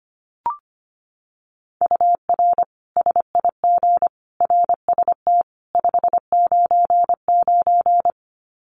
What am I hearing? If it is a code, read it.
Morse code, heard as a keyed tone, about 25 wpm, 700 Hz. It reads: UR SIG RST 599
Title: Common QSO Phrases 25wpm